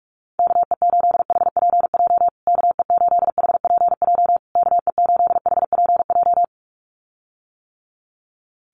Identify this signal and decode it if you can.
Morse code, heard as a keyed tone, about 44 wpm, 700 Hz. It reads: KE8HPJ KE8HPJ KE8HPJ
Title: USA Call Signs Repeated 3x 45wpm